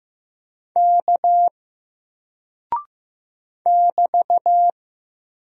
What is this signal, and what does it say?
Morse code, heard as a keyed tone, about 15 wpm, 700 Hz. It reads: K <BT>
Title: Single Character (Letter:Number:Punctuation:Prosign) Alt 15wpm